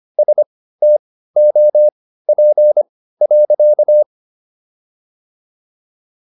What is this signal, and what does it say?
Morse code, heard as a keyed tone, about 25 wpm, spaced as for 15 wpm, 600 Hz. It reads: STOP.